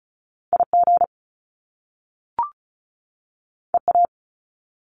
Morse code, heard as a keyed tone, about 36 wpm, 700 Hz. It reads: IG EA
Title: Top Two Letter Combinations in Words 35wpm